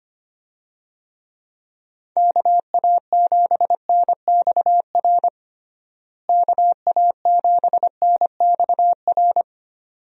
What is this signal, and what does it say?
Morse code, heard as a keyed tone, about 25 wpm, 700 Hz. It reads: KA7NXR KA7NXR
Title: USA Call Signs - 6 Characters - Repeated 2x - 25wpm